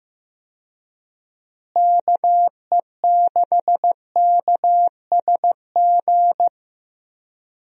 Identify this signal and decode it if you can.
Morse code, heard as a keyed tone, about 15 wpm, 700 Hz. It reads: KE6KSG
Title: USA Call Signs - 6 Characters - Repeated 2x - 15wpm